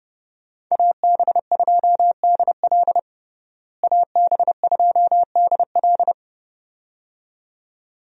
Morse code, heard as a keyed tone, about 30 wpm, 700 Hz. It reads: AB2DL AB2DL
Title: USA Call Signs - 5 Characters - Repeated 2x - 30wpm